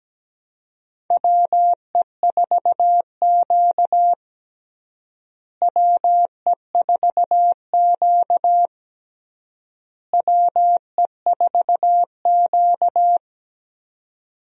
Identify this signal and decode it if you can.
Morse code, heard as a keyed tone, about 17 wpm, 700 Hz. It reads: WE4Q WE4Q WE4Q
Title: USA Call Signs - 4 Characters - Repeated 3x - 17wpm